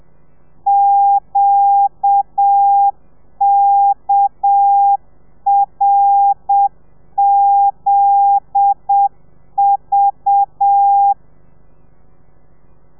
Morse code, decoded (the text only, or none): QKRZV